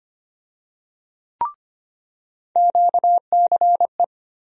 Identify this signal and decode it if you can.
Morse code, heard as a keyed tone, about 25 wpm, 700 Hz. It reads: QCE
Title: USA Call Sign Postfixes - 3 Letters 25wpm